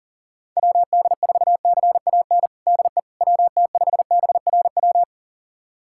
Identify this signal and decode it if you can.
Morse code, heard as a keyed tone, about 40 wpm, 700 Hz. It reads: WD4CAN DE WT5BRW
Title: USA Callsign DE Callsign 40wpm